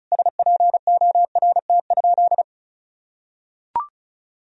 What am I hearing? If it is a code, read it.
Morse code, heard as a keyed tone, about 36 wpm, 700 Hz. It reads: SPORT?
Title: Sentences from Top 1000 Words - Review Part 4 - 35wpm